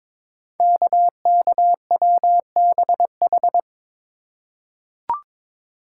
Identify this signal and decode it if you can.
Morse code, heard as a keyed tone, about 22 wpm, 700 Hz. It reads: KKWBH
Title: Code Groups - 5 Letters - 22wpm